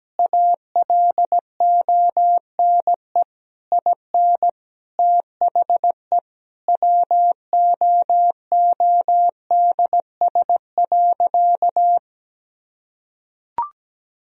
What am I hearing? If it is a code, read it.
Morse code, heard as a keyed tone, about 17 wpm, 700 Hz. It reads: ALONE IN THE WOODS.
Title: Sentences from Top 1000 Words - Review Part 2 - 17wpm